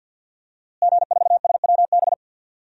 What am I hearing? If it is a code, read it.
Morse code, heard as a keyed tone, about 51 wpm, 700 Hz. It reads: G4SWB
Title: DXCC-100 Call Signs - 5 Characters - Repeated 2x - 50wpm